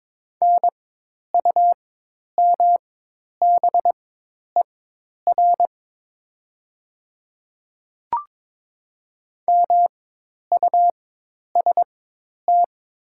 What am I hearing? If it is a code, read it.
Morse code, heard as a keyed tone, about 22 wpm, spaced as for 10 wpm, 700 Hz. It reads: NUMBER MUST
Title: 4x Character Spacing - Top 300 Words - 22wpm